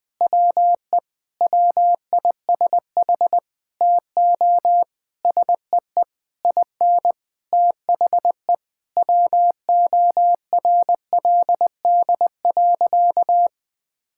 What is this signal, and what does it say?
Morse code, heard as a keyed tone, about 20 wpm, 700 Hz. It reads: WE WISH TO SEE IN THE WORLD.
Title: Sentences with 8-Word Phrases - Spoken Beginning - 20wpm